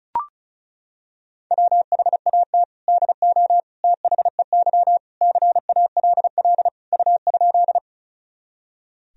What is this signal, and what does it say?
Morse code, heard as a keyed tone, about 36 wpm, 700 Hz. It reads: WHAT DO THEY CALL U?
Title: Sentences from Top 100 Words and Abbreviations 35wpm